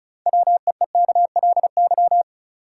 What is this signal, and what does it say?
Morse code, heard as a keyed tone, about 34 wpm, 700 Hz. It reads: WEEKLY
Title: Six Letter Words - 35wpm